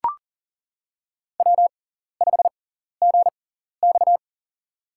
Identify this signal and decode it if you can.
Morse code, heard as a keyed tone, about 40 wpm, spaced as for 14 wpm, 700 Hz. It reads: W5GX
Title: USA Call Signs - 4 Characters - 6x Character Spacing - 40wpm